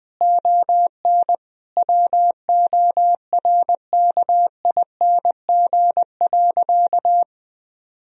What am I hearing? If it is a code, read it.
Morse code, heard as a keyed tone, about 20 wpm, 700 Hz. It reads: ON WORKING.
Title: Sentences from Top 1000 Words - Review Part 4 - 20wpm